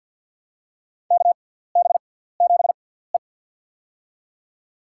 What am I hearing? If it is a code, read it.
Morse code, heard as a keyed tone, about 52 wpm, spaced as for 17 wpm, 690 Hz. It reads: KB7E
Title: USA Call Signs - 4 Characters - 6x Character Spacing - 50wpm